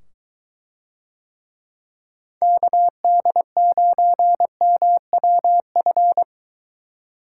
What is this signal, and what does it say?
Morse code, heard as a keyed tone, about 22 wpm, 700 Hz. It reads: KD9MWF